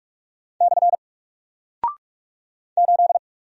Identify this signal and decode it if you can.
Morse code, heard as a keyed tone, about 46 wpm, 700 Hz. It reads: / 8
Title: Single Character (Letter:Number:Punctuation:Prosign) 45wpm